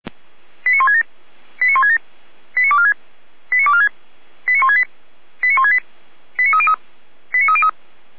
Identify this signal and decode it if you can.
Five-tone selective call (ZVEI), heard as quick runs of five tones, tones of about 70 ms. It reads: 89167 89167 89246 89246 89168 89168 89303 89303